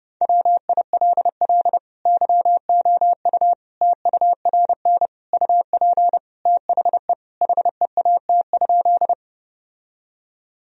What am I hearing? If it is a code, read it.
Morse code, heard as a keyed tone, about 29 wpm, 700 Hz. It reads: WILL YOU TURN UP THE HEAT?
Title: Sentences from Top 1000 Words - Review Part 3 - 30wpm